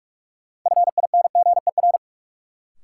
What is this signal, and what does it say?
Morse code, heard as a keyed tone, about 45 wpm, 700 Hz. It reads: FINGER
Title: Sentences from Top 1600 Words - New Word Drill - 45wpm